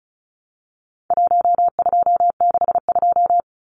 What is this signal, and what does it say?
Morse code, heard as a keyed tone, about 34 wpm, 700 Hz. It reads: 1262
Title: Numbers - 4-Digit 35wpm